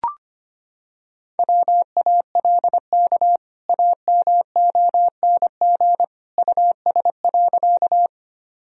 Morse code, heard as a keyed tone, about 25 wpm, 700 Hz. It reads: WALK AMONG US.
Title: Sentences from Top 1000 Words - Review Part 3 - 25wpm